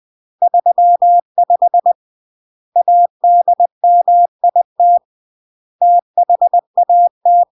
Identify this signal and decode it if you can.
Morse code, heard as a keyed tone, about 20 wpm, 700 Hz. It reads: 35 ADMIT THAT
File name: CW / jodorowsky / 82 commandments 20-20